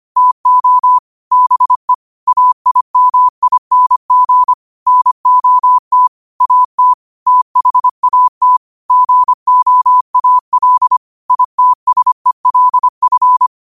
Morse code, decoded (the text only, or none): TO BE AIMING NOT AT THAT GOAL ITSELF